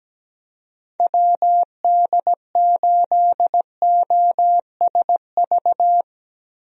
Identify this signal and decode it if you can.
Morse code, heard as a keyed tone, about 17 wpm, 700 Hz. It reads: WD8OSV
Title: USA Call Signs - 6 Characters - Repeated 3x - 17wpm